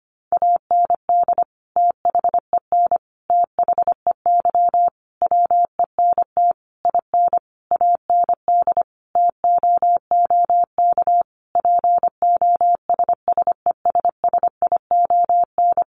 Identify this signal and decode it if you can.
Morse code, heard as a keyed tone, about 25 wpm, 700 Hz. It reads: AND THEN THEY WENT IN AND TOOK POSSESSION
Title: Sentences from Top 1400 Words 25wpm